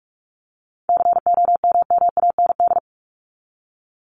Morse code, heard as a keyed tone, about 44 wpm, 700 Hz. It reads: COMMAND
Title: Top 1501-1750 Words 45wpm